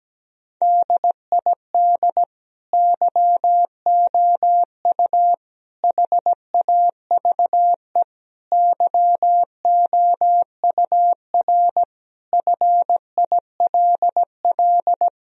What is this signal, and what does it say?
Morse code, heard as a keyed tone, about 17 wpm, 700 Hz. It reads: DID YOU HAVE YOUR FILL
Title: Sentences from Top 1000 Words - Review Part 3 - 17wpm